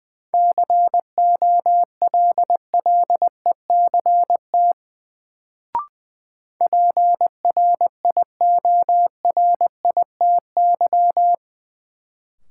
Morse code, heard as a keyed tone, about 20 wpm, 700 Hz. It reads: COLLECT PRIORITY
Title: Sentences from Top 1600 Words - New Word Drill Alt - 20wpm